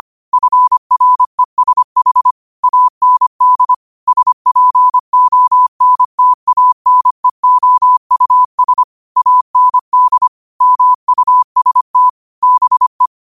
Morse code, heard as a keyed tone, about 25 wpm, 1000 Hz. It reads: FRESH AND SPONTANEOUS AND MUST BE
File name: 25wpm